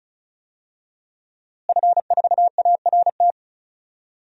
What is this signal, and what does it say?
Morse code, heard as a keyed tone, about 34 wpm, 700 Hz. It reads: F4ART
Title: DXCC-100 Call Signs - 5 Characters - Repeated 2x - 35wpm